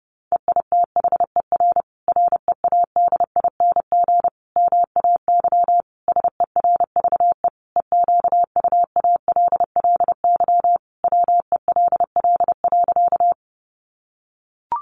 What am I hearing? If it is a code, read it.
Morse code, heard as a keyed tone, about 28 wpm, 700 Hz. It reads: EITHER READING MAY SERVE EQUALLY WELL.
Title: Sentences from Top 1400 Words 30wpm